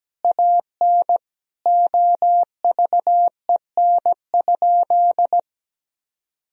Morse code, heard as a keyed tone, about 17 wpm, 700 Hz. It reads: AN OVEN?
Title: Sentences from Top 1000 Words - Review Part 4 - 17wpm